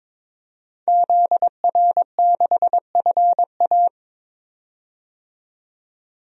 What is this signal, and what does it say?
Morse code, heard as a keyed tone, about 22 wpm, 700 Hz. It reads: ZR6FA
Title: DXCC-100 Call Signs - 5 Characters - 22wpm